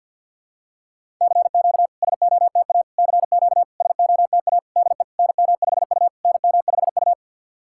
Morse code, heard as a keyed tone, about 50 wpm, 690 Hz. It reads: CQ SOTA CQ SOTA DE NM5U NM5U
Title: SOTA QSO Practice 50wpm